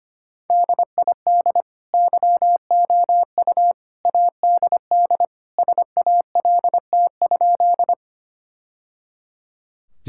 Morse code, heard as a keyed tone, about 25 wpm, 700 Hz. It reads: DID YOU ADD SALT?
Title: Sentences from Top 1000 Words - Review Part 5 - 25wpm